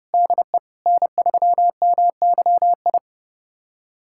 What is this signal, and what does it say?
Morse code, heard as a keyed tone, about 30 wpm, 700 Hz. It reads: DE N3MYI
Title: USA Callsign DE Callsign - 5 Characters - 30wpm